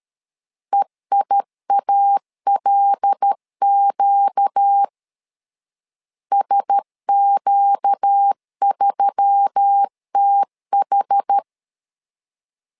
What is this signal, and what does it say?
Morse code, heard as a keyed tone, about 13 wpm, 800 Hz. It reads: EIALQ SQ3TH
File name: LEZIONI / 5 LBVC / 5 L 12wpm 1913